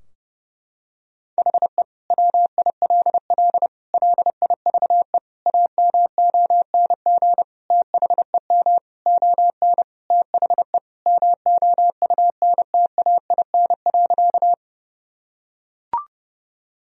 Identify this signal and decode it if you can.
Morse code, heard as a keyed tone, about 32 wpm, 700 Hz. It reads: HE WILL LIVE AMONG THEM ON THE MOUNTAIN.
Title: Sentences from Top 1000 Words - Review Part 3 - 30wpm